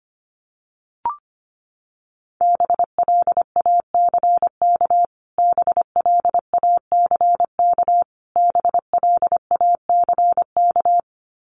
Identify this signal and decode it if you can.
Morse code, heard as a keyed tone, about 25 wpm, 700 Hz. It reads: BLACK BLACK BLACK
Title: Top 500 Words Repeated 25wpm